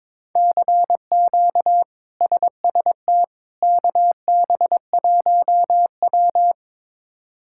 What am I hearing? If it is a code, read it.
Morse code, heard as a keyed tone, about 22 wpm, 700 Hz. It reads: CQ SST KB1W
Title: K1USN SST QSO Practice - 22wpm